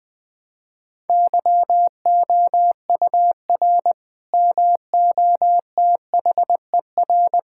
Morse code, heard as a keyed tone, about 21 wpm, 700 Hz. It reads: YOUR MOTHER
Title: Sentences from Top 1600 Words 20wpm